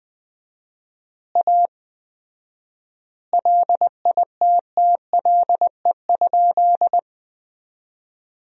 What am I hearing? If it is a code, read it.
Morse code, heard as a keyed tone, about 20 wpm, 700 Hz. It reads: A LITTLE?
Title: 4x Word-Spacing - Sentences from Top 500 Words - 20wpm